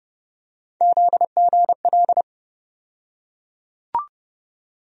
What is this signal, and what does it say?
Morse code, heard as a keyed tone, about 30 wpm, 700 Hz. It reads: ZGL
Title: USA Call Sign Postfixes - 3 Letters 30wpm